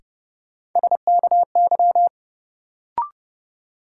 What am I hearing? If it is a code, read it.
Morse code, heard as a keyed tone, about 30 wpm, 700 Hz. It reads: SKY
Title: Sentences from Top 1200 Words - New Word Drill Alt - 30wpm